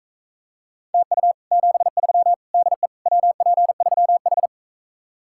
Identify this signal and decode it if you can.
Morse code, heard as a keyed tone, about 43 wpm, 700 Hz. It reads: TU 73 DE WP3H